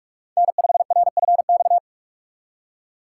Morse code, heard as a keyed tone, about 46 wpm, 700 Hz. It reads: N5RFX
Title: USA Call Signs - 5 Characters - 2x Character Spacing - 45wpm